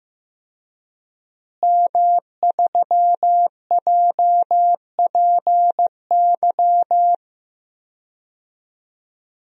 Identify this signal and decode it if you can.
Morse code, heard as a keyed tone, about 15 wpm, 700 Hz. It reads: M3JPY
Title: DXCC-100 Call Signs - 5 Characters - Repeated 3x - 15wpm